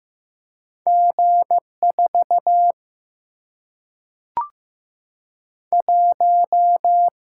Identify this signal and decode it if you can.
Morse code, heard as a keyed tone, about 15 wpm, 700 Hz. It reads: G4 1
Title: Single-Letter Single-Number 15wpm